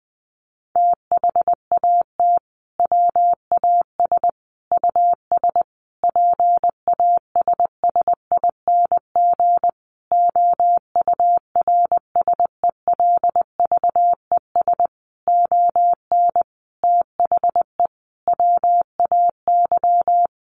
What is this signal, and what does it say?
Morse code, heard as a keyed tone, about 19 wpm, 700 Hz. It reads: THAT WAS US PASSING OURSELVES ON THE WAY
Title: A Sound Of Thunder 20wpm